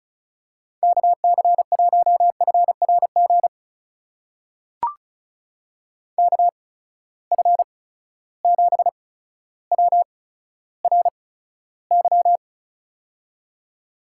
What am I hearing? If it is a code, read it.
Morse code, heard as a keyed tone, about 34 wpm, spaced as for 15 wpm, 700 Hz. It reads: KC1FRG KF7WRY
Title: USA Call Signs - 6 Characters - 8x Character Spacing - 35wpm